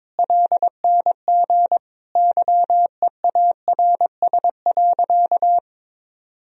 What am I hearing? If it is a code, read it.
Morse code, heard as a keyed tone, about 22 wpm, 700 Hz. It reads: LNG YEARS.
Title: Sentences from Top 200 Words and Abbreviations 22wpm